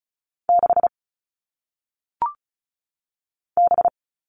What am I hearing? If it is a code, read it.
Morse code, heard as a keyed tone, about 36 wpm, 700 Hz. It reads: 6 B
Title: All Cut Numbers - 1-Digit 35wpm